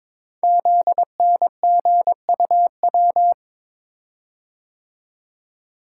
Morse code, heard as a keyed tone, about 22 wpm, 700 Hz. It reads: ZNGUW
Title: Code Groups - 5 Letters - 22wpm